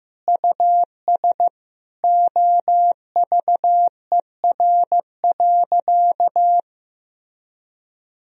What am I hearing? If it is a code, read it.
Morse code, heard as a keyed tone, about 15 wpm, 700 Hz. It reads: US OVER.
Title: Sentences from Top 1000 Words - Review Part 1 - 15wpm